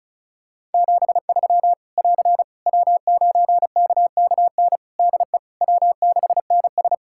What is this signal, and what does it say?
Morse code, heard as a keyed tone, about 36 wpm, 700 Hz. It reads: 73 <AR> W9KKN DE W6NS
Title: Practice QSO - Exchange 4 - Ending - 35wpm